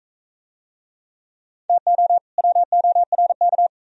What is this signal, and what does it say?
Morse code, heard as a keyed tone, about 40 wpm, 700 Hz. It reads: TO WORK